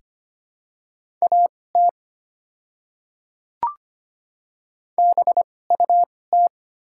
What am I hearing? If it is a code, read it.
Morse code, heard as a keyed tone, about 25 wpm, spaced as for 18 wpm, 700 Hz. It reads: AT BUT
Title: 2x Character Spacing - Top 300 Words - 25wpm